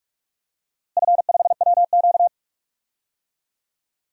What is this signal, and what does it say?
Morse code, heard as a keyed tone, about 41 wpm, 700 Hz. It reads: F5WQ
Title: DXCC-100 Call Signs - 4 Characters - Repeated 2x - 45wpm